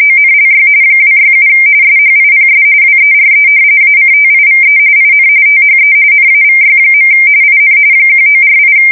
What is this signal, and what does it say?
RTTY, about 45.45 baud. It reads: ONLY 267 SHIPS WERE ACTIVE IN THE NAVY. IN 1948
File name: fsk170 / usnavy5